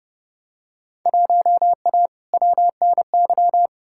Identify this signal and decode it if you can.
Morse code, heard as a keyed tone, about 32 wpm, 700 Hz. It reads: 1A WNY
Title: Field Day Exchanges Encore - 30wpm